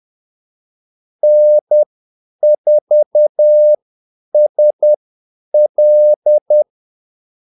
Morse code, heard as a keyed tone, about 10 wpm, spaced as for 8 wpm, 600 Hz. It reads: N4SL
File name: Callsigns-10-8wpm